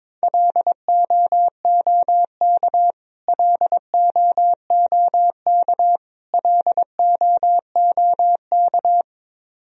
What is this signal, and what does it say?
Morse code, heard as a keyed tone, about 22 wpm, 700 Hz. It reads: LOOK LOOK LOOK
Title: Top 100 Words Repeated 22wpm